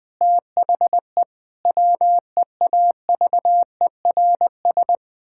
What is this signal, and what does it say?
Morse code, heard as a keyed tone, about 20 wpm, 700 Hz. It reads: THE WEAVERS
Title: Pangram Challenge v2 - 20wpm